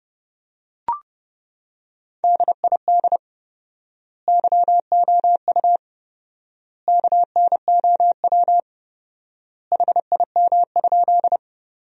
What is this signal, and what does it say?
Morse code, heard as a keyed tone, about 30 wpm, 700 Hz. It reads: DID YOU KNOW HIM?
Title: 4x Word-Spacing - Sentences from Top 100 Words - 30wpm